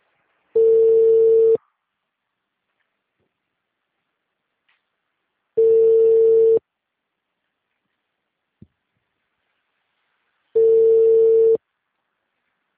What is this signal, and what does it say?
Ringback tone (China, 450 Hz).
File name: out- -2023- - - .21